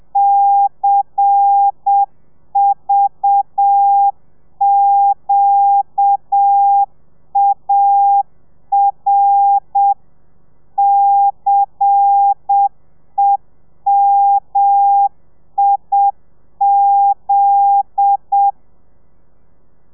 Morse code, heard as a keyed tone, about 7 wpm, 800 Hz. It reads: CVQAR CEMIZ
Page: Lezione 1: audio a 30 CPM